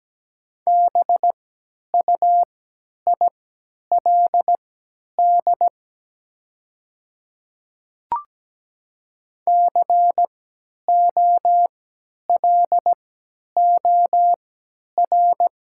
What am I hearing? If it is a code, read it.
Morse code, heard as a keyed tone, about 17 wpm, spaced as for 10 wpm, 700 Hz. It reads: BUILD COLOR
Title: 3x Character Spacing - Top 300 Words - 17wpm